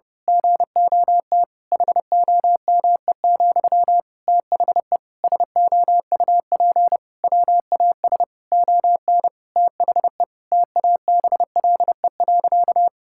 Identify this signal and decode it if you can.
Morse code, heard as a keyed tone, about 30 wpm, 700 Hz. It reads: GOT HOME, THE SOUP WAS ON THE TABLE.
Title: Sentences from Top 1000 Words - Review Part 6 - 30wpm